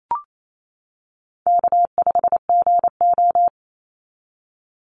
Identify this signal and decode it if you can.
Morse code, heard as a keyed tone, about 28 wpm, 700 Hz. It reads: K5GO
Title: Easy Call Signs - 28wpm